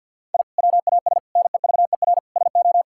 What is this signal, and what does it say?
Morse code, heard as a keyed tone, about 50 wpm, 700 Hz. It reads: I WAS NEVER SO
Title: Sentences from Top 1000 Words - Review Part 5 - 50wpm